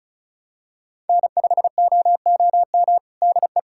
Morse code, heard as a keyed tone, about 34 wpm, 690 Hz. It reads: N5OOM DE
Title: USA Callsign DE Callsign 35wpm